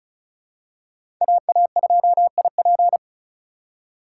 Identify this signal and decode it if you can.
Morse code, heard as a keyed tone, about 34 wpm, 700 Hz. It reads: AA2IP
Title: USA Call Signs - 5 Characters - 2x Character Spacing - 35wpm